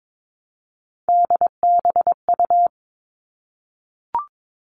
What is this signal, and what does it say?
Morse code, heard as a keyed tone, about 22 wpm, 700 Hz. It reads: DBU
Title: All Cut Numbers - 3-Digit 22wpm